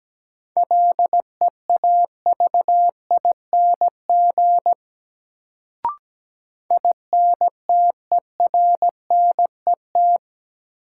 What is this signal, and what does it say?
Morse code, heard as a keyed tone, about 17 wpm, 700 Hz. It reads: LEAVING INTERNET
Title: Sentences from Top 1000 Words - Review Part 4 - New Word Drill - 17wpm